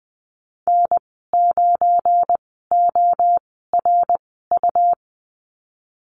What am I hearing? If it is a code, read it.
Morse code, heard as a keyed tone, about 20 wpm, spaced as for 15 wpm, 700 Hz. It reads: N9ORU
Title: 186 Lesson 30g - Letter b - Call Signs 20wpm